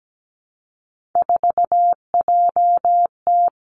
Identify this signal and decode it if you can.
Morse code, heard as a keyed tone, about 17 wpm, 700 Hz. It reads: 4JT
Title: Three Challenging Letter-Number Combinations 17wpm